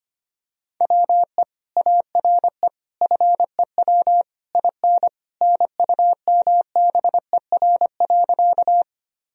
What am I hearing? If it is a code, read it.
Morse code, heard as a keyed tone, about 25 wpm, 700 Hz. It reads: WE ARE FEW IN NUMBER.
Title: Sentences from Top 1000 Words - Review Part 1 - 25wpm